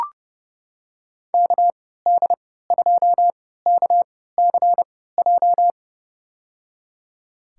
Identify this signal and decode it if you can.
Morse code, heard as a keyed tone, about 29 wpm, spaced as for 17 wpm, 700 Hz. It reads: KD2KCJ
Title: USA Call Signs - 6 Characters - 3x Character Spacing - 30wpm